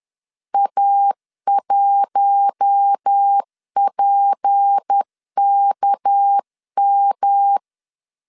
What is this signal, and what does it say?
Morse code, heard as a keyed tone, about 10 wpm, 800 Hz. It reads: A1PKM
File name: LEZIONI / 7 GF9Z / 7 G 10wpm 2501